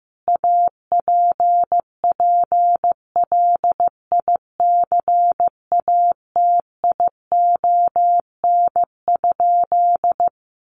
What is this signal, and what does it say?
Morse code, heard as a keyed tone, about 15 wpm, 700 Hz. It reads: APPLICATION?
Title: Sentences from Top 1200 Words 15wpm